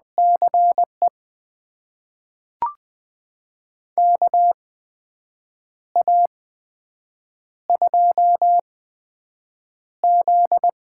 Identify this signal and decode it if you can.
Morse code, heard as a keyed tone, about 20 wpm, spaced as for 7 wpm, 700 Hz. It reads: CE KA2Z
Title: USA Call Signs - 6 Characters - 8x Character Spacing - 20wpm